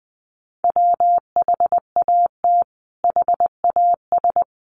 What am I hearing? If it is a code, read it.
Morse code, heard as a keyed tone, about 20 wpm, 700 Hz. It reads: WHAT HAS
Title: Sentences from Top 600 Words 20wpm